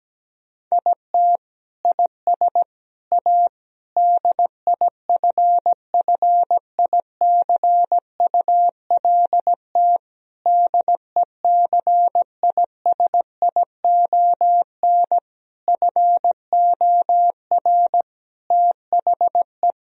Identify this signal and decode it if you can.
Morse code, heard as a keyed tone, about 17 wpm, 700 Hz. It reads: IT IS A DIFFICULT DECISION FOR THE
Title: Sentences from Top 1000 Words - Review Part 4 - 17wpm